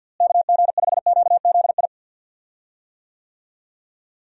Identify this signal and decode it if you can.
Morse code, heard as a keyed tone, about 53 wpm, 700 Hz. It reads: KG5QZI
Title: USA Call Signs - 6 Characters - Repeated 3x - 50wpm